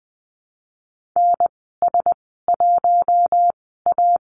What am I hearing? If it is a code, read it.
Morse code, heard as a keyed tone, about 20 wpm, spaced as for 14 wpm, 700 Hz. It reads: NS1A
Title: 082 Lesson 14g - Number 2 - Call Signs 20wpm